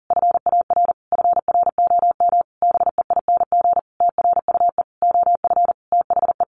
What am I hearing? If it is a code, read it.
Morse code, heard as a keyed tone, about 40 wpm, 700 Hz. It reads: FAR FROM BEING TRUE OF THE
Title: Sentences from Top 1400 Words 40wpm